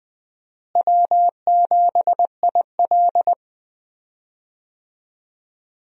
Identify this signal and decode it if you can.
Morse code, heard as a keyed tone, about 20 wpm, 700 Hz. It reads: W7IL
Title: USA Call Signs - 4 Characters - Repeated 2x - 20wpm